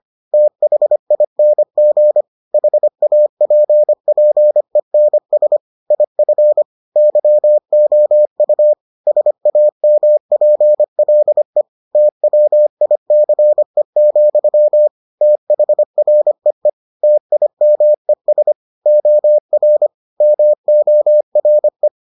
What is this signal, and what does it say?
Morse code, heard as a keyed tone, about 25 wpm, 600 Hz. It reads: THING HAPPENS IF YOU SAMPLE TWICE, THREE TIMES OR MORE